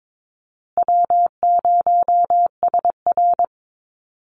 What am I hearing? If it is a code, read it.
Morse code, heard as a keyed tone, about 22 wpm, 700 Hz. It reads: W0SR
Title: USA Call Signs Repeated 3x 22wpm